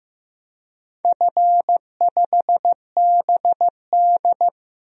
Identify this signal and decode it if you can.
Morse code, heard as a keyed tone, about 15 wpm, 700 Hz. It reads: F5BD
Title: DXCC-100 Call Signs - 4 Characters - Repeated 2x - 15wpm